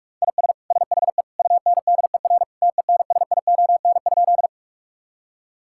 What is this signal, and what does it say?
Morse code, heard as a keyed tone, about 46 wpm, 700 Hz. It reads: IS SHE UNDER TENSION?
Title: Sentences from Top 1600 Words 45wpm